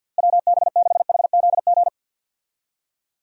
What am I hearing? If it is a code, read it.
Morse code, heard as a keyed tone, about 50 wpm, 700 Hz. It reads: WB6HZG
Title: USA Call Signs - 6 Characters - Repeated 3x - 50wpm